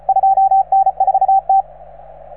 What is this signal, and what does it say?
Morse code, heard as a keyed tone, about 35 wpm, 770 Hz, with radio noise. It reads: 2N4T